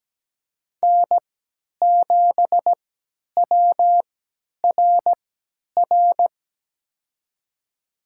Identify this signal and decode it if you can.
Morse code, heard as a keyed tone, about 17 wpm, spaced as for 10 wpm, 700 Hz. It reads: N7WRR
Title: USA Call Signs - 5 Characters - 3x Character Spacing - 17wpm